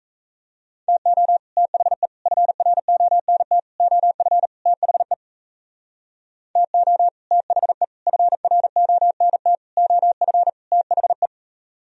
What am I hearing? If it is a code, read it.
Morse code, heard as a keyed tone, about 41 wpm, 700 Hz. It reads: TO THE FRONT OF THE TO THE FRONT OF THE